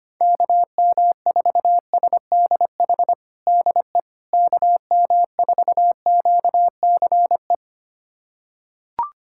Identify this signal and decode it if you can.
Morse code, heard as a keyed tone, about 25 wpm, 700 Hz. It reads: KM4SDH DE KM4QCE
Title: USA Callsign DE Callsign 25wpm